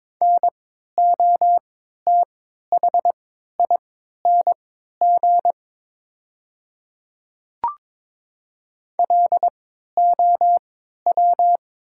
Morse code, heard as a keyed tone, about 22 wpm, spaced as for 12 wpm, 700 Hz. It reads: NOTHING LOW
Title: 3x Character Spacing - Top 500 Words - 22wpm